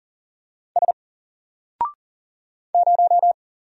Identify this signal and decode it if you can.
Morse code, heard as a keyed tone, about 38 wpm, 700 Hz. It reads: S 0
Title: Single Letter-Number 40wpm